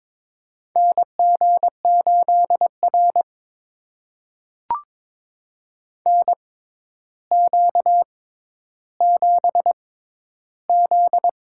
Morse code, heard as a keyed tone, about 22 wpm, spaced as for 11 wpm, 700 Hz. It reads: NG8R NQ7Z
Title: USA Call Signs - 4 Characters - 6x Character Spacing - 22wpm